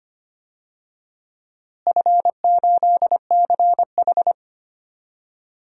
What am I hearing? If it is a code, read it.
Morse code, heard as a keyed tone, about 25 wpm, 700 Hz. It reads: F8CH
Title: DXCC-100 Call Signs - 4 Characters - Repeated 3x - 25wpm